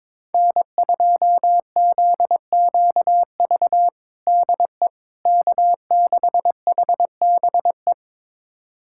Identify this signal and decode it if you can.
Morse code, heard as a keyed tone, about 22 wpm, 700 Hz. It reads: N2ZQV DE K6HBE
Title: USA Callsign DE Callsign 22wpm